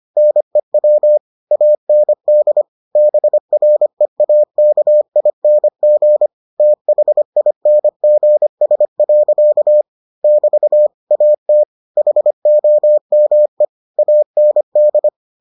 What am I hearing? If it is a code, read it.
Morse code, heard as a keyed tone, about 25 wpm, 600 Hz. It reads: NEW AND BREAKING THINGS. <BT> AT HOME AND